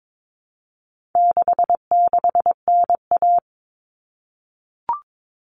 Morse code, heard as a keyed tone, about 22 wpm, 700 Hz. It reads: 66NA
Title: All Cut Numbers - 4-Digit 22wpm